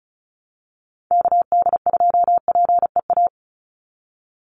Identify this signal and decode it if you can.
Morse code, heard as a keyed tone, about 34 wpm, 700 Hz. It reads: KD2PEA
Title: USA Call Signs 35wpm